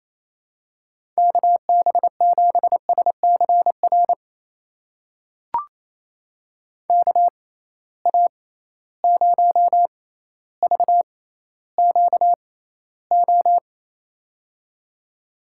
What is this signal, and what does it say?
Morse code, heard as a keyed tone, about 28 wpm, spaced as for 14 wpm, 700 Hz. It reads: KB7SCR KA0VQO
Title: USA Call Signs - 6x Character Spacing - 28wpm